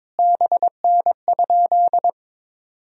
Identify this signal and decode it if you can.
Morse code, heard as a keyed tone, about 22 wpm, 700 Hz. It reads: BN?
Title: Sentences from Top 200 Words and Abbreviations 22wpm